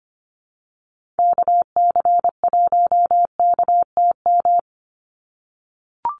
Morse code, heard as a keyed tone, about 25 wpm, 700 Hz. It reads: KC1KTM
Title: USA Call Signs 25wpm